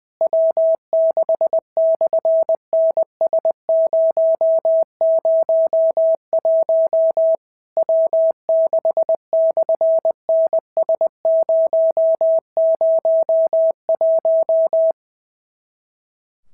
Morse code, heard as a keyed tone, about 20 wpm, 650 Hz. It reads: W6/NS001 W6/NS001
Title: SOTA QSO Practice 20wpm